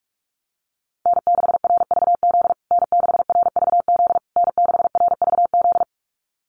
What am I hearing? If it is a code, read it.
Morse code, heard as a keyed tone, about 44 wpm, 700 Hz. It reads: N6RVZ N6RVZ N6RVZ
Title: USA Call Signs Repeated 3x 45wpm